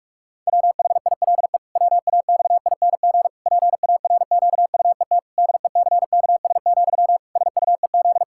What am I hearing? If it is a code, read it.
Morse code, heard as a keyed tone, about 41 wpm, 700 Hz. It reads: WHILE WAXING PARQUET DECKS, SUEZ
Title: Pangram Challenge v2 - 45wpm